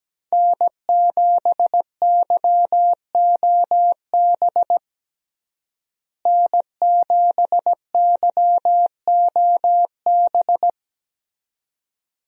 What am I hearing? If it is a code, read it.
Morse code, heard as a keyed tone, about 17 wpm, 700 Hz. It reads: N7YOB N7YOB
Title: USA Call Signs - 5 Characters - Repeated 2x - 17wpm